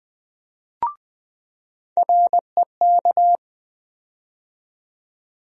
Morse code, heard as a keyed tone, about 21 wpm, 700 Hz. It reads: REK
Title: USA Call Sign Postfixes - 3 Letters 20wpm